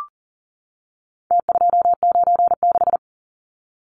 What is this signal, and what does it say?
Morse code, heard as a keyed tone, about 40 wpm, 700 Hz. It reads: T296
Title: Numbers - Contest Style - 4-Digit 40wpm